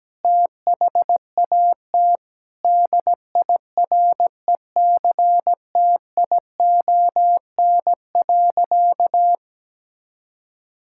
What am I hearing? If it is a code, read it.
Morse code, heard as a keyed tone, about 17 wpm, 700 Hz. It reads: THAT DIRECTION.
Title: Sentences from Top 1000 Words - Review Part 5 - 17wpm